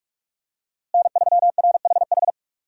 Morse code, heard as a keyed tone, about 41 wpm, 700 Hz. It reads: N3RHH
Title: USA Call Signs - 5 Characters - Repeated 2x - 45wpm